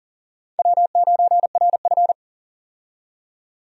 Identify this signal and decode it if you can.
Morse code, heard as a keyed tone, about 40 wpm, 700 Hz. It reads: W9RF
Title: USA Call Signs - 4 Characters - 40wpm